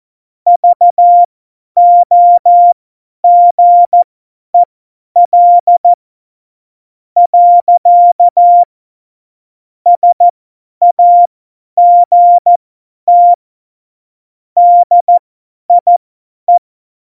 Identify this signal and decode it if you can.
Morse code, heard as a keyed tone, about 14 wpm, spaced as for 9 wpm, 700 Hz. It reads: VOGEL. SAGT DIE
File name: Morsekurs / Witze / Witz 04. September 2018, 50 F70